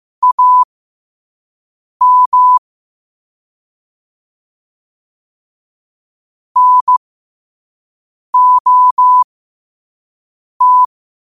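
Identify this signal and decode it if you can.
Morse code, heard as a keyed tone, about 15 wpm, spaced as for 5 wpm, 1000 Hz. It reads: AM NOT
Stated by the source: Morse at 5 wpm.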